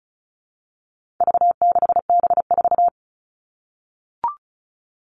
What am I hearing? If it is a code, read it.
Morse code, heard as a keyed tone, about 34 wpm, 700 Hz. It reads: V6B4
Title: All Cut Numbers - 4-Digit 35wpm